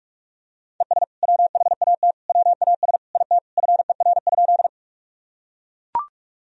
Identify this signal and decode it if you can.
Morse code, heard as a keyed tone, about 42 wpm, 700 Hz. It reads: ES WHAT WAS IT FER?
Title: Sentences from Top 100 Words and Abbreviations 45wpm